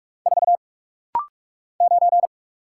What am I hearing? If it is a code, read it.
Morse code, heard as a keyed tone, about 48 wpm, 690 Hz. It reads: 4 9
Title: Single Letter-Number - Rapid-Fire 45wpm